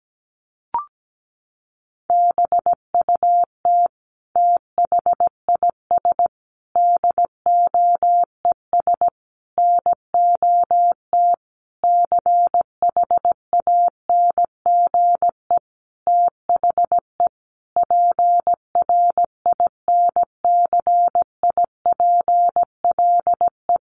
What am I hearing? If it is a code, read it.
Morse code, heard as a keyed tone, about 17 wpm, 700 Hz. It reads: BUT THIS DOES NOT CHANGE THE PRINCIPLE
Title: Sentences from Top 1400 Words 17wpm